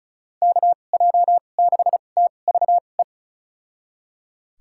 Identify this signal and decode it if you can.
Morse code, heard as a keyed tone, about 34 wpm, spaced as for 25 wpm, 700 Hz. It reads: KJ6TVE
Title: USA Call Signs - 2x Character Spacing - 35wpm